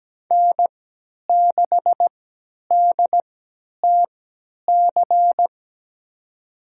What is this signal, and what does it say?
Morse code, heard as a keyed tone, about 17 wpm, spaced as for 10 wpm, 700 Hz. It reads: N6DTC
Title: USA Call Signs - 5 Characters - 3x Character Spacing - 17wpm